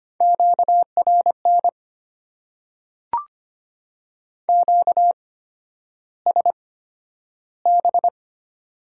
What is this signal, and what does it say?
Morse code, heard as a keyed tone, about 25 wpm, spaced as for 11 wpm, 700 Hz. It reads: QRN QSB
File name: Top 90 QSO Elements - 8x Character Spacing - 25wpm